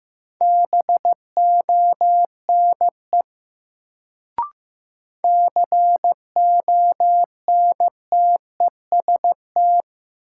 Morse code, heard as a keyed tone, about 15 wpm, 700 Hz. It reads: BONE CONTEST
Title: Sentences from Top 1600 Words - New Word Drill Alt - 15wpm